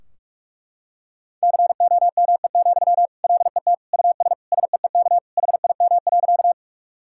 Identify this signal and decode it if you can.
Morse code, heard as a keyed tone, about 41 wpm, 700 Hz. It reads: COME, LET US SEEK HIM.
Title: Sentences from Top 1600 Words 45wpm